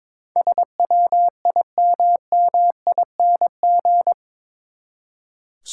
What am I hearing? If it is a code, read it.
Morse code, heard as a keyed tone, about 21 wpm, 700 Hz. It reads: SWIMMING
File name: practice / 22wpm / Eight Letter Words - 22wpm